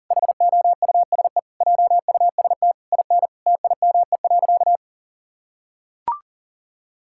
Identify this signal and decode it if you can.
Morse code, heard as a keyed tone, about 37 wpm, 700 Hz. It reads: HOUSE JUST IN TIME.
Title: Sentences from Top 1600 Words 40wpm